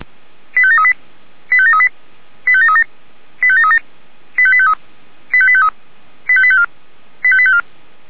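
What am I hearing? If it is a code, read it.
Five-tone selective call (ZVEI), heard as quick runs of five tones, tones of about 70 ms. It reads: 85738 85738 85737 85737 85853 85853 85854 85854
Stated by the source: ZVEI five-tone selective call received by SDR.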